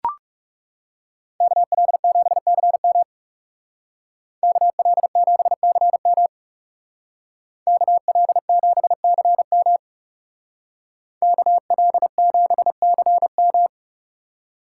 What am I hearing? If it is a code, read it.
Morse code, heard as a keyed tone, about 35 wpm, 700 Hz. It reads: KL7CM KL7CM KL7CM KL7CM
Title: Speed Racing - USA 2x2 Call Signs 30wpm